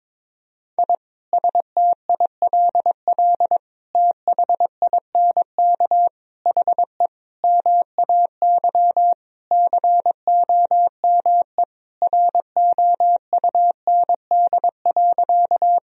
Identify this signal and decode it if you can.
Morse code, heard as a keyed tone, about 22 wpm, 700 Hz. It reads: I STILL THINK HE MAY COME ROUND.
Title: Sentences from Top 1000 Words - Review Part 1 - 22wpm